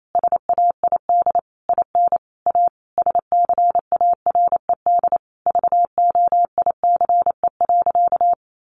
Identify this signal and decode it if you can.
Morse code, heard as a keyed tone, about 28 wpm, 700 Hz. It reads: SAID IN A SCARED VOICE.
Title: Sentences from Top 700 Words 28wpm